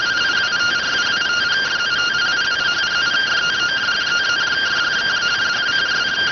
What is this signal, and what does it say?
Two-tone teleprinter signal, about 170 Hz shift.